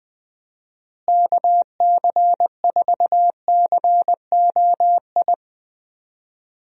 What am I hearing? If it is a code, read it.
Morse code, heard as a keyed tone, about 21 wpm, 700 Hz. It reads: KC4COI
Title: USA Call Signs - 6 Characters - Repeated 2x - 20wpm